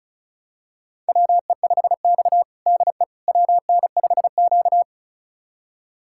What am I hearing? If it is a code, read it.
Morse code, heard as a keyed tone, about 34 wpm, 700 Hz. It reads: WE5X DE WN5Q
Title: USA Callsign DE Callsign - 4 Characters - 35wpm